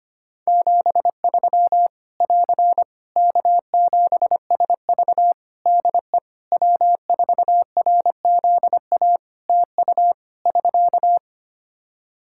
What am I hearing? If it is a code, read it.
Morse code, heard as a keyed tone, about 25 wpm, 700 Hz. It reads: 73 <AR> K7SV DE W4RZA TU <SK>
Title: Practice QSO - Exchange 4 - Ending - 25wpm